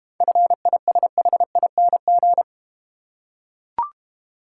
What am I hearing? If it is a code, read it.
Morse code, heard as a keyed tone, about 29 wpm, 700 Hz. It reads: FISHING
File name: Easy Words - Speed-Racing with Hints - 25wpm